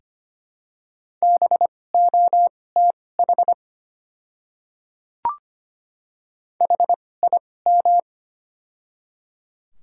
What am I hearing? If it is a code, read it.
Morse code, heard as a keyed tone, about 25 wpm, spaced as for 18 wpm, 700 Hz. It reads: BOTH HIM
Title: 2x Character Spacing - Top 300 Words - 25wpm